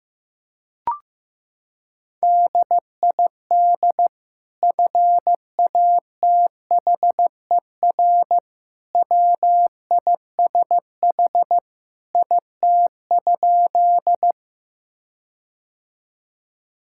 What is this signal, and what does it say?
Morse code, heard as a keyed tone, about 15 wpm, 700 Hz. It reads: DID FATHER WISH IT?
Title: Sentences from Top 1600 Words 15wpm